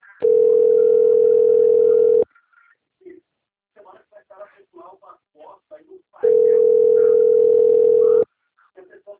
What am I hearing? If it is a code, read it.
Ringback tone (North America), 440 + 480 Hz, with speech.
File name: rg-1600-unknown- - - .15039